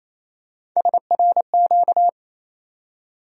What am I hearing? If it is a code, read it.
Morse code, heard as a keyed tone, about 28 wpm, 700 Hz. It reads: SRQ
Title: USA Call Sign Postfixes - 3 Letters 28wpm